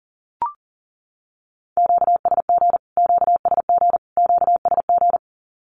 Morse code, heard as a keyed tone, about 40 wpm, 700 Hz. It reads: QSG QSG QSG
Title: Q-Codes Repeated 3x 40wpm